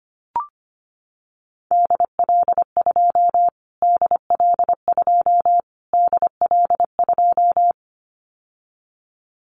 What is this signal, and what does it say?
Morse code, heard as a keyed tone, about 25 wpm, 700 Hz. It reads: DL2 DL2 DL2
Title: Callsign Prefixes for Top 25 DXCC - Repeated 3x - 25wpm